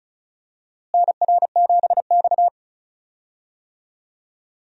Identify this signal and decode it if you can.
Morse code, heard as a keyed tone, about 36 wpm, 700 Hz. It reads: NR7X
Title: USA Call Signs - 4 Characters - Repeated 3x - 35wpm